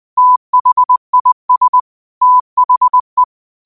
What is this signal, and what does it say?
Morse code, heard as a keyed tone, about 20 wpm, 1000 Hz. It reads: THIS THE
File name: qotdfiles / 20wpm